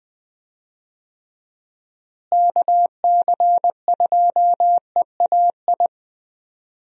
Morse code, heard as a keyed tone, about 20 wpm, 700 Hz. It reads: KC2EAI
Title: USA Call Signs - 6 Characters - Repeated 2x - 20wpm